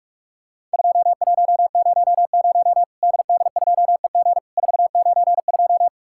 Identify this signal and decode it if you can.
Morse code, heard as a keyed tone, about 41 wpm, 700 Hz. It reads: 2100 DD2EG 492
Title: Worked All Europe DX Contest - QTC Exchange - 45wpm